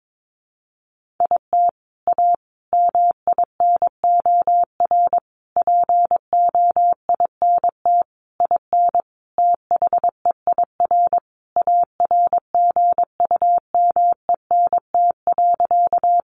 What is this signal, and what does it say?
Morse code, heard as a keyed tone, about 22 wpm, 700 Hz. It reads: IT A MINOR POINT IN THEIR ARGUMENT.
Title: Sentences from Top 1500 Words 22wpm